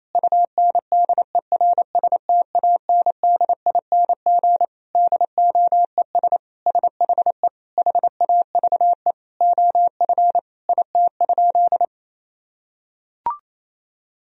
Morse code, heard as a keyed tone, about 28 wpm, 700 Hz. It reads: UNDERSTANDING DOES SHE HAVE OF IT?
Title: Sentences from Top 1000 Words - Review Part 3 - 28wpm